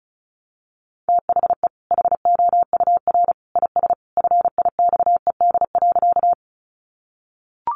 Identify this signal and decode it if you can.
Morse code, heard as a keyed tone, about 34 wpm, 700 Hz. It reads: THE HOUR IS FIXED.
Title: Sentences from Top 1100 Words 35wpm